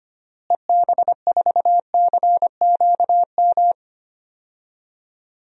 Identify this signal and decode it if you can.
Morse code, heard as a keyed tone, about 25 wpm, 700 Hz. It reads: EB4CQM
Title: DXCC-100 Call Signs - 6 Characters - Repeated 3x - 25wpm